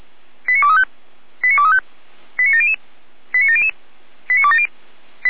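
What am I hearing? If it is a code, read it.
Five-tone selective call (ZVEI), heard as quick runs of five tones, tones of about 80 ms. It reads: 89235 89235 89700 89700 89270
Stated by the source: Five-tone ZVEI selective call 89187, received by SDR.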